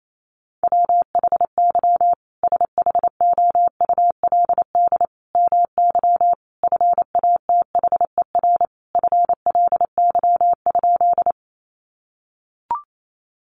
Morse code, heard as a keyed tone, about 28 wpm, 700 Hz. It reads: WHY SHOULD MY FATHER FLY?
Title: Sentences from Top 500 Words 28wpm